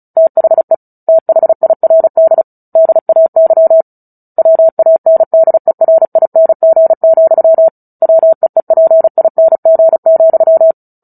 Morse code, heard as a keyed tone, about 34 wpm, 650 Hz. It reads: THE THIRD DAY WANDERING, WEEPING,
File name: Book Two / 09-Chapter 9 - 35 WPM